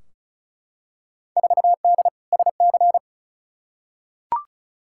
Morse code, heard as a keyed tone, about 34 wpm, 700 Hz. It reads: 4D SC
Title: Field Day Exchanges Encore - 35wpm